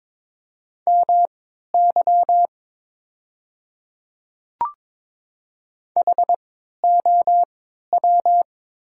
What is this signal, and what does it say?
Morse code, heard as a keyed tone, about 22 wpm, spaced as for 13 wpm, 700 Hz. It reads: MY HOW
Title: 3x Character Spacing - Top 100 Words - 22wpm